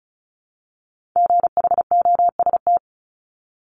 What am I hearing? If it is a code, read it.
Morse code, heard as a keyed tone, about 36 wpm, 700 Hz. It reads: GHOST
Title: Multi-letter Phonemes 35wpm